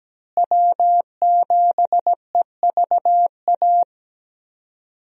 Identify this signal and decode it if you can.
Morse code, heard as a keyed tone, about 17 wpm, 700 Hz. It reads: W7EVA
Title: USA Call Signs - 5 Characters - Repeated 3x - 17wpm